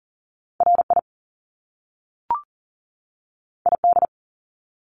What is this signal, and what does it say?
Morse code, heard as a keyed tone, about 40 wpm, 700 Hz. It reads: RI ID
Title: Top Two Letter Combinations in Words 40wpm